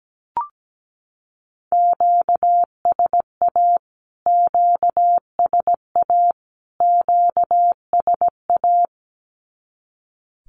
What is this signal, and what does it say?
Morse code, heard as a keyed tone, about 17 wpm, 700 Hz. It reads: QSA QSA QSA
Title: Q-Codes Repeated 3x 17wpm